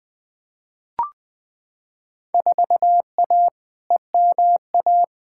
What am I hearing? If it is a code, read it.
Morse code, heard as a keyed tone, about 20 wpm, 700 Hz. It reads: 4A EMA
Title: Field Day Exchanges - 20wpm